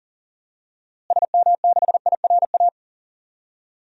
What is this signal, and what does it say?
Morse code, heard as a keyed tone, about 40 wpm, 700 Hz. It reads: SM6IRA
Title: DXCC-100 Call Signs - 6 Characters - Repeated 3x - 40wpm